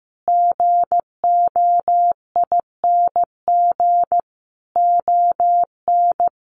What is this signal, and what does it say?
Morse code, heard as a keyed tone, about 15 wpm, 700 Hz. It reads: GOING ON
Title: Sets of 4 Words 15wpm